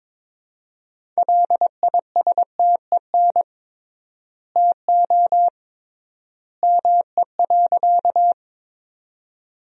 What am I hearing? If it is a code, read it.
Morse code, heard as a keyed tone, about 22 wpm, 700 Hz. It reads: LISTEN TO ME.
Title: 3x Word-Spacing - Sentences from Top 400 Words - 22wpm